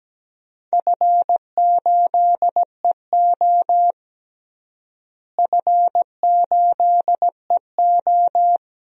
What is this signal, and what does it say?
Morse code, heard as a keyed tone, about 17 wpm, 700 Hz. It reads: F8EO F8EO
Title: DXCC-100 Call Signs - 4 Characters - Repeated 3x - 17wpm